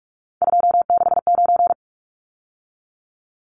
Morse code, heard as a keyed tone, about 44 wpm, 700 Hz. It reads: 269
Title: Numbers - 3-Digit 45wpm